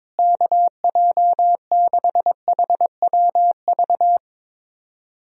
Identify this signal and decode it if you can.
Morse code, heard as a keyed tone, about 22 wpm, 700 Hz. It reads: KJ6HWV
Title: USA Call Signs - 6 Characters - Repeated 2x - 22wpm